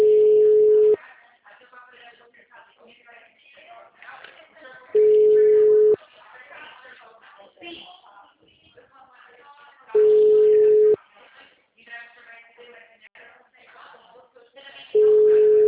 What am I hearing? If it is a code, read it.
Ringback tone (Europe, 425 Hz), with speech.